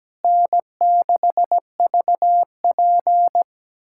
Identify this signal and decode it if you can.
Morse code, heard as a keyed tone, about 17 wpm, 700 Hz. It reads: N6VP
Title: USA Callsign DE Callsign - 4 Characters - 17wpm